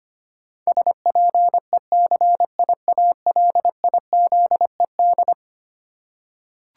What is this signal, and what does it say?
Morse code, heard as a keyed tone, about 25 wpm, 700 Hz. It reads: SPECIALIZED
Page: morse-code-ninja-practice-25wpm